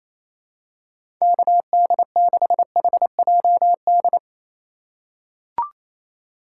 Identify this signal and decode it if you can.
Morse code, heard as a keyed tone, about 27 wpm, 700 Hz. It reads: KD6HJD
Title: USA Call Signs - 6 Characters - 4x Character Spacing - 28wpm